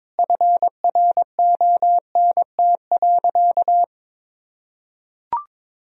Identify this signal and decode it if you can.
Morse code, heard as a keyed tone, about 22 wpm, 700 Hz. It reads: FRONT.
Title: Sentences from Top 1000 Words - Review Part 2 - 22wpm